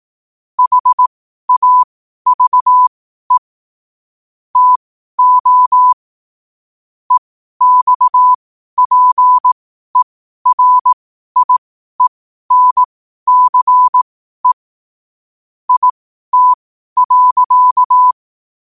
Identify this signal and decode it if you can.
Morse code, heard as a keyed tone, about 18 wpm, spaced as for 12 wpm, 1000 Hz. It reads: HAVE TO EXPERIENCE IT.
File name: qotdfiles / 12wpm